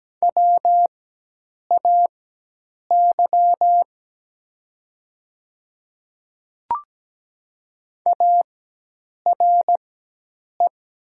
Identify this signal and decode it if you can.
Morse code, heard as a keyed tone, about 17 wpm, spaced as for 8 wpm, 700 Hz. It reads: WAY ARE
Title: 4x Character Spacing - Top 100 Words - 17wpm